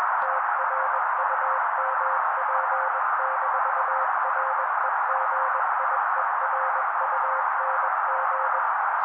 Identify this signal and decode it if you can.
Morse code, heard as a keyed tone, about 21 wpm, 530 Hz. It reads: TRUMP-REGIERUNG